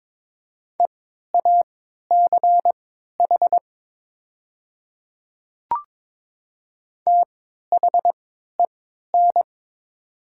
Morse code, heard as a keyed tone, about 22 wpm, spaced as for 12 wpm, 700 Hz. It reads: EACH THEN
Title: 3x Character Spacing - Top 100 Words - 22wpm